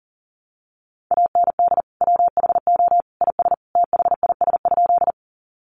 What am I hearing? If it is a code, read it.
Morse code, heard as a keyed tone, about 40 wpm, 700 Hz. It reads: AND WHO IS THIS?
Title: Sentences from Top 100 Words 40wpm